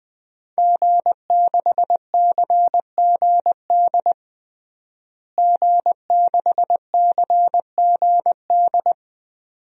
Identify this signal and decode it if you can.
Morse code, heard as a keyed tone, about 20 wpm, 700 Hz. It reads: G6CGD G6CGD
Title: DXCC-100 Call Signs - 5 Characters - Repeated 2x - 20wpm